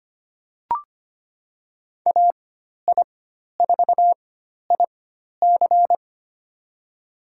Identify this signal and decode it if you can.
Morse code, heard as a keyed tone, about 25 wpm, spaced as for 12 wpm, 700 Hz. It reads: AI4IC
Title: USA Call Signs - 5 Characters - 4x Character Spacing - 25wpm